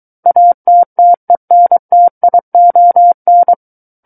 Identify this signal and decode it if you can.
Morse code, heard as a keyed tone, about 23 wpm, 700 Hz. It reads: ATTENTION
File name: Book One / 14-Chapter 14 - 23 WPM